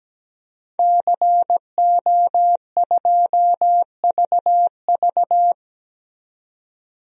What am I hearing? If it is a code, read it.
Morse code, heard as a keyed tone, about 17 wpm, 700 Hz. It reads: CO2VV
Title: DXCC-100 Call Signs - 5 Characters - 17wpm